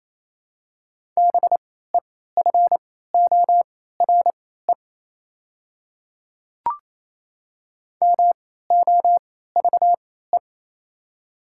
Morse code, heard as a keyed tone, about 28 wpm, spaced as for 16 wpm, 690 Hz. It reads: BEFORE MOVE
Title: 3x Character Spacing - Top 200 Words - 28wpm